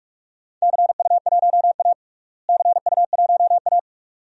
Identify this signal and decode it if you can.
Morse code, heard as a keyed tone, about 42 wpm, 700 Hz. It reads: CU1A CU1A
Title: DXCC-100 Call Signs - 4 Characters - Repeated 2x - 45wpm